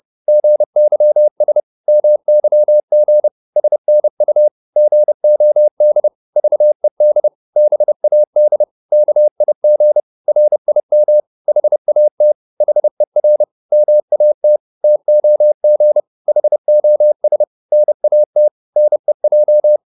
Morse code, heard as a keyed tone, about 30 wpm, 600 Hz. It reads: GYS MYG SNU GOD VED BAD KIG RIM HAT HER MAT TOG HOS NAT NEJ
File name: Korte ord DK 30wpm